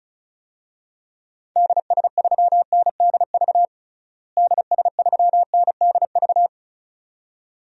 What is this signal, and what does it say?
Morse code, heard as a keyed tone, about 36 wpm, 700 Hz. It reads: DS3NDV DS3NDV
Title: DXCC-100 Call Signs - 6 Characters - Repeated 2x - 35wpm